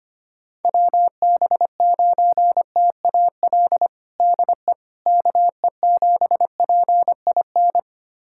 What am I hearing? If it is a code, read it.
Morse code, heard as a keyed tone, about 25 wpm, 700 Hz. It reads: WB9TAL DE KE7PIN
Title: USA Callsign DE Callsign - 6 Characters - 25wpm